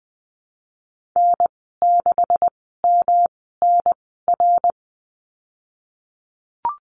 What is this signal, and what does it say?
Morse code, heard as a keyed tone, about 20 wpm, spaced as for 14 wpm, 700 Hz. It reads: N6MNR
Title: 173 Lesson 28g - Number 9 - Call Signs 20wpm